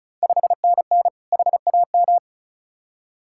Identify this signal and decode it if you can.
Morse code, heard as a keyed tone, about 36 wpm, 700 Hz. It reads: 5NN HAM
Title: Ontario QSO Party Exchange 35wpm